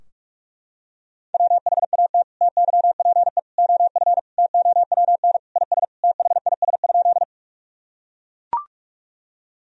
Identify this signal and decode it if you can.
Morse code, heard as a keyed tone, about 42 wpm, 700 Hz. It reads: WHAT TYPE OF TOWN IS THIS?
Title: Sentences from Top 1000 Words - Review Part 3 - 45wpm